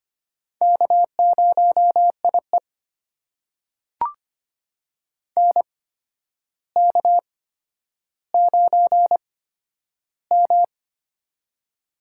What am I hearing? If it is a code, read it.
Morse code, heard as a keyed tone, about 25 wpm, spaced as for 11 wpm, 700 Hz. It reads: K0IE NK9M
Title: USA Call Signs - 4 Characters - 8x Character Spacing - 25wpm